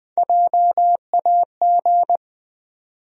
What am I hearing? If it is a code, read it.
Morse code, heard as a keyed tone, about 20 wpm, 700 Hz. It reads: JAG
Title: USA Call Sign Postfixes - 3 Letters 20wpm